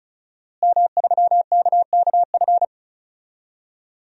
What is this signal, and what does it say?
Morse code, heard as a keyed tone, about 36 wpm, 700 Hz. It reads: M3KKF
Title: DXCC-100 Call Signs - 5 Characters - Repeated 3x - 35wpm